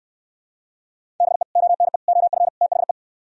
Morse code, heard as a keyed tone, about 68 wpm, 700 Hz. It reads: BE ONE OF THE